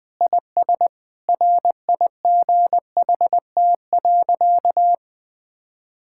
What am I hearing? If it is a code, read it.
Morse code, heard as a keyed tone, about 20 wpm, 700 Hz. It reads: IS RIGHT.
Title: Sentences with 5-Word Phrases - Spoken Beginning - 20wpm